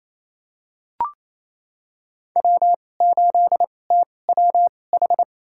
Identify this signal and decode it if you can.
Morse code, heard as a keyed tone, about 28 wpm, spaced as for 20 wpm, 700 Hz. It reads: W8TWH
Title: USA Call Signs - 2x Character Spacing - 28wpm